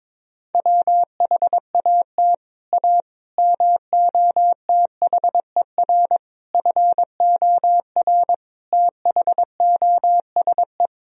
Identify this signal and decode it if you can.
Morse code, heard as a keyed tone, about 21 wpm, 700 Hz. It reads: WHAT A MOTHER FOR THOSE
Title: Sentences from Top 1000 Words - Review Part 1 - 22wpm